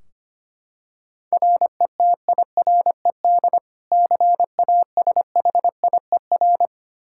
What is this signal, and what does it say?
Morse code, heard as a keyed tone, about 25 wpm, 700 Hz. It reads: RETIRED CASHIER
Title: Practice QSO - Exchange 3 - More Info - 25wpm